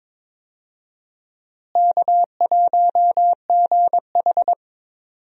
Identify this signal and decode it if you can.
Morse code, heard as a keyed tone, about 22 wpm, 700 Hz. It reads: K1GH
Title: USA Call Signs - 4 Characters - Repeated 3x - 22wpm